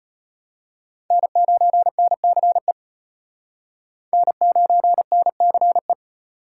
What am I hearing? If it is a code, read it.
Morse code, heard as a keyed tone, about 33 wpm, 700 Hz. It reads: N9NCE N9NCE